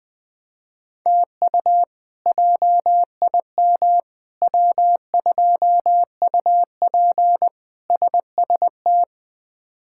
Morse code, heard as a keyed tone, about 20 wpm, 700 Hz. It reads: TU JIM W2UP SST
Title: K1USN SST QSO Practice - 20wpm